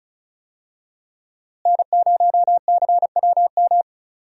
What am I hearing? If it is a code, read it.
Morse code, heard as a keyed tone, about 34 wpm, 700 Hz. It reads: N0CWM
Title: USA Call Signs - 5 Characters - Repeated 3x - 35wpm